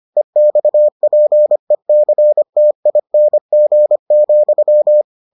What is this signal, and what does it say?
Morse code, heard as a keyed tone, about 25 wpm, 600 Hz. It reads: EXPECTING,